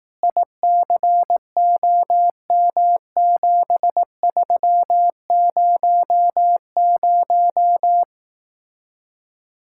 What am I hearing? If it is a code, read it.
Morse code, heard as a keyed tone, about 18 wpm, 700 Hz. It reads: ICOM7300
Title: Speed-Racing Common QSO Phrases Encore - 15wpm